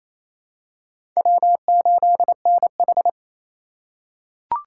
Morse code, heard as a keyed tone, about 28 wpm, 700 Hz. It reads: W8NH
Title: USA Call Signs - 3x Character Spacing - 28wpm